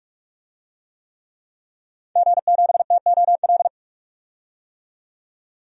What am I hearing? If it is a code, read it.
Morse code, heard as a keyed tone, about 46 wpm, 700 Hz. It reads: G7TOL
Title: DXCC-100 Call Signs - 5 Characters - 45wpm